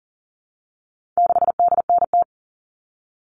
Morse code, heard as a keyed tone, about 40 wpm, 700 Hz. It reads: 6DNT
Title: All Cut Numbers - 4-Digit 40wpm